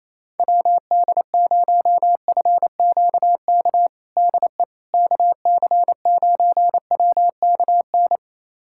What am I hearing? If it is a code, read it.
Morse code, heard as a keyed tone, about 28 wpm, 700 Hz. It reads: WD0FQK DE KC9WKN
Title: USA Callsign DE Callsign 28wpm